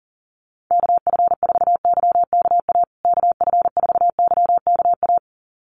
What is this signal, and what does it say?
Morse code, heard as a keyed tone, about 40 wpm, 700 Hz. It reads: KF4YKA KF4YKA
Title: USA Call Signs Repeated 2x 40wpm